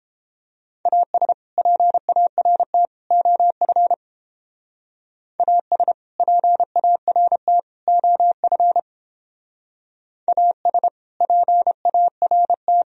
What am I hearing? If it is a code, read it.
Morse code, heard as a keyed tone, about 31 wpm, 700 Hz. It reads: AS PART OF AS PART OF AS PART